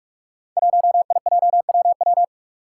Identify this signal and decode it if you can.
Morse code, heard as a keyed tone, about 44 wpm, 700 Hz. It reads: 1IJWW
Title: Five Challenging Letter-Number Combinations 45wpm